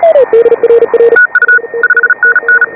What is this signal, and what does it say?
Morse code, heard as a keyed tone, about 34 wpm, 1470 Hz, with noise. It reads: TU 5NN
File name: AUDIO2018 / HA5BA CW